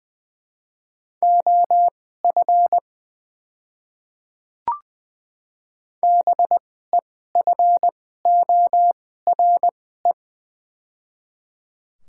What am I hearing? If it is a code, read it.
Morse code, heard as a keyed tone, about 20 wpm, spaced as for 14 wpm, 700 Hz. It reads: OF BEFORE
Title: 2x Character Spacing - Top 200 Words - 20wpm